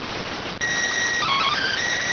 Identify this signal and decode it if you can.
SSTV (slow-scan television), PD 290.